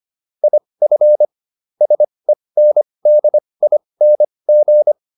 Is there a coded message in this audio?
Yes: Morse code from 0.4 to 4.9 s, heard as a keyed tone, about 25 wpm, spaced as for 20 wpm, 600 Hz. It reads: IF SENDING